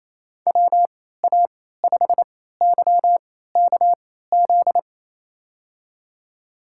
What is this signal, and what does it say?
Morse code, heard as a keyed tone, about 28 wpm, spaced as for 16 wpm, 700 Hz. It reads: WA5YKZ
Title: USA Call Signs - 6 Characters - 3x Character Spacing - 28wpm